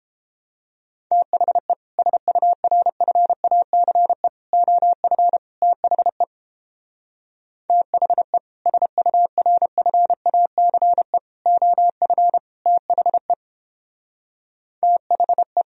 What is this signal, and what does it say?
Morse code, heard as a keyed tone, about 29 wpm, 700 Hz. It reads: THE SURFACE OF THE THE SURFACE OF THE THE